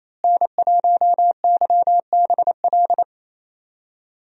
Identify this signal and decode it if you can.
Morse code, heard as a keyed tone, about 28 wpm, 700 Hz. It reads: N1YBL
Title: USA Call Signs - 5 Characters - Repeated 2x - 28wpm